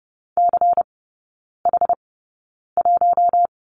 Morse code, heard as a keyed tone, about 30 wpm, spaced as for 9 wpm, 690 Hz. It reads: CH1
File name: Lesson 16e - Letter c - Three Characters 30wpm